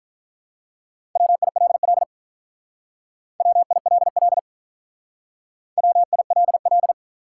Morse code, heard as a keyed tone, about 44 wpm, 700 Hz. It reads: WILL WILL WILL